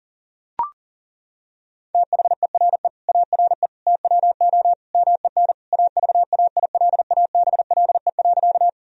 Morse code, heard as a keyed tone, about 40 wpm, 700 Hz. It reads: THERE ARE TWO MEN AVAILABLE.
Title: Sentences from Top 1000 Words - Review Part 3 - 40wpm